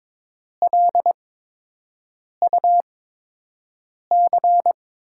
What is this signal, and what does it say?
Morse code, heard as a keyed tone, about 22 wpm, spaced as for 6 wpm, 700 Hz. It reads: LUC
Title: USA Call Signs - 5 Characters - 8x Character Spacing - 22wpm